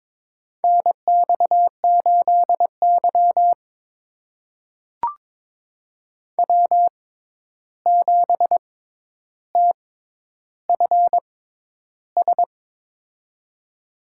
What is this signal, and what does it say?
Morse code, heard as a keyed tone, about 22 wpm, spaced as for 11 wpm, 700 Hz. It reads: NX8Y W7TFS
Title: USA Call Signs - 6x Character Spacing - 22wpm